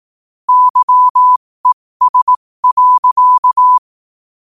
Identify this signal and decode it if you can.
Morse code, heard as a keyed tone, about 18 wpm, spaced as for 15 wpm, 1000 Hz. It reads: YES.
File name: qotdfiles / 15wpm